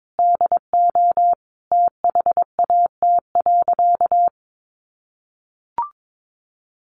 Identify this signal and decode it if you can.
Morse code, heard as a keyed tone, about 22 wpm, 700 Hz. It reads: DO THAT.
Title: Sentences from Top 1200 Words 22wpm